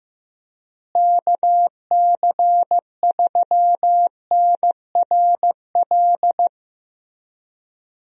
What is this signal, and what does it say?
Morse code, heard as a keyed tone, about 15 wpm, 700 Hz. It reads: KC3NRL
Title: USA Call Signs - 6 Characters - Repeated 3x - 15wpm